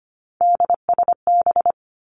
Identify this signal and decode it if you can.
Morse code, heard as a keyed tone, about 25 wpm, 700 Hz. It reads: DSB
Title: Three Challenging Letter-Number Combinations 25wpm